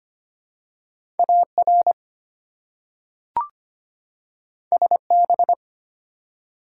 Morse code, heard as a keyed tone, about 25 wpm, 700 Hz. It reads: AR SB
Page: morse-code-ninja-practice-25wpm